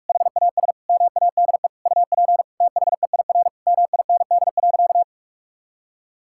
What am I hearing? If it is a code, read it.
Morse code, heard as a keyed tone, about 47 wpm, 700 Hz. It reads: HAS MADE UP THEIR MIND.
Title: Sentences from Top 1000 Words - Review Part 5 - 45wpm